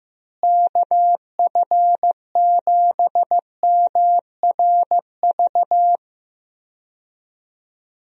Morse code, heard as a keyed tone, about 15 wpm, 700 Hz. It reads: KF7MRV
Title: USA Call Signs - 6 Characters - Repeated 2x - 15wpm